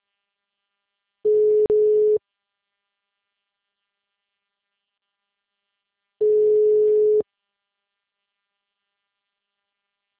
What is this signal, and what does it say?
Ringback tone (Europe, 425 Hz).